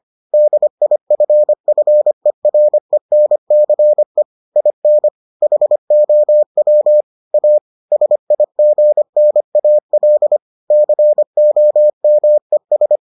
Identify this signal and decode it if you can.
Morse code, heard as a keyed tone, about 25 wpm, 600 Hz. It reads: DIFFERENCE IN HOW A SIGNAL COMES